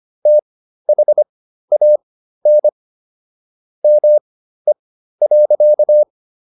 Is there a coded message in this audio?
Yes: Morse code from 0.3 to 6.0 s, heard as a keyed tone, about 25 wpm, spaced as for 13 wpm, 600 Hz. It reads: THAN ME.